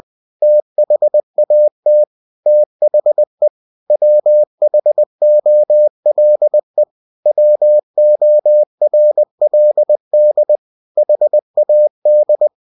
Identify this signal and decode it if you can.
Morse code, heard as a keyed tone, about 20 wpm, 600 Hz. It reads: THAT THE WHOLE WORLD HAD